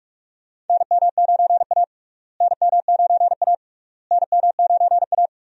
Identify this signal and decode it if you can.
Morse code, heard as a keyed tone, about 43 wpm, 700 Hz. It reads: NM9A NM9A NM9A
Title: USA Call Signs - 4 Characters - Repeated 3x - 45wpm